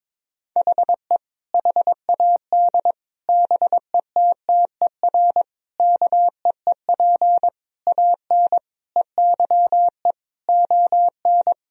Morse code, heard as a keyed tone, about 22 wpm, 700 Hz. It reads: HE HAD BETTER KEEP AN EYE ON
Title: Sentences from Top 1000 Words - Review Part 1 - 22wpm